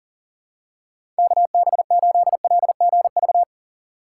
Morse code, heard as a keyed tone, about 40 wpm, 700 Hz. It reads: KB8LGV
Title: USA Call Signs - 6 Characters - 3x Character Spacing - 40wpm